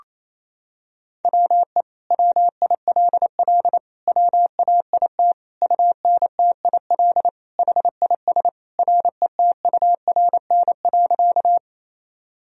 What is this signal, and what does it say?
Morse code, heard as a keyed tone, about 28 wpm, 700 Hz. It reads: WE WILL WAIT UNTIL HIS RETURN.
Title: Sentences from Top 1000 Words - Review Part 1 - 28wpm